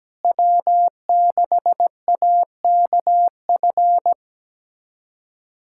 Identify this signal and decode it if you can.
Morse code, heard as a keyed tone, about 17 wpm, 700 Hz. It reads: W6AKF
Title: USA Callsign DE Callsign - 5 Characters - 17wpm